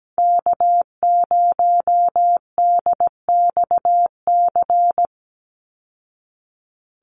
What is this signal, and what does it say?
Morse code, heard as a keyed tone, about 17 wpm, 700 Hz. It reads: K0DXC
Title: Call Signs Repeated 3x - 17wpm